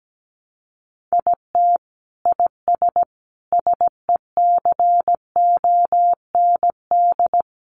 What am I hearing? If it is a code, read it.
Morse code, heard as a keyed tone, about 17 wpm, 700 Hz. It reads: IT IS SECOND
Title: Sentences from Top 600 Words 17wpm